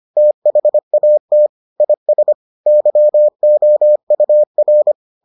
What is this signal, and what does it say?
Morse code, heard as a keyed tone, about 25 wpm, 600 Hz. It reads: THAT IS YOUR